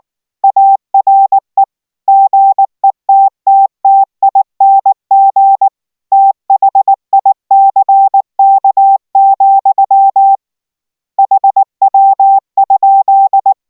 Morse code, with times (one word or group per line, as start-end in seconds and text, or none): 0.4-1.6 ARE
2.1-5.7 GETTTING
6.1-10.3 THICK,
11.2-13.5 HW?